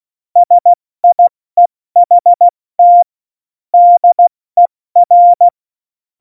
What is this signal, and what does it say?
Morse code, heard as a keyed tone, about 16 wpm, 700 Hz. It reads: SIEHT DER